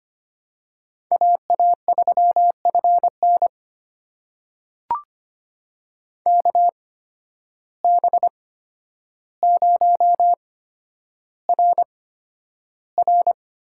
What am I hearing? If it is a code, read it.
Morse code, heard as a keyed tone, about 25 wpm, spaced as for 11 wpm, 700 Hz. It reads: AA3FN KB0RR
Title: USA Call Signs - 8x Character Spacing - 25wpm